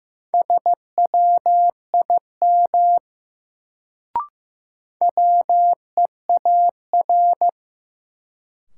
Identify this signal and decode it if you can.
Morse code, heard as a keyed tone, about 15 wpm, 700 Hz. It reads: SWIM WEAR
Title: Sentences from Top 1000 Words - Review Part 6 - New Word Drill - 15wpm